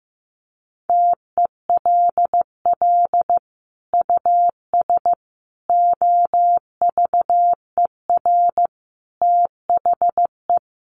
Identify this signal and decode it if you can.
Morse code, heard as a keyed tone, about 15 wpm, 700 Hz. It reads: TELL US OVER THE
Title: Sentences from Top 1100 Words 15wpm